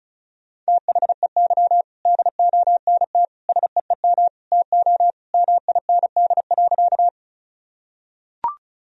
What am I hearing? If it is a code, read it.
Morse code, heard as a keyed tone, about 36 wpm, 700 Hz. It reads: THEY DONT SEEM TO MIND.
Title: Sentences from Top 1000 Words - Review Part 1 - 35wpm